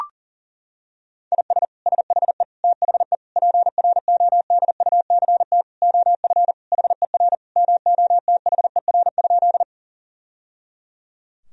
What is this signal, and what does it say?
Morse code, heard as a keyed tone, about 37 wpm, 700 Hz. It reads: IS SHE THE PRODUCT OF HER MOTHER?
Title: Sentences from Top 1000 Words - Review Part 2 - 40wpm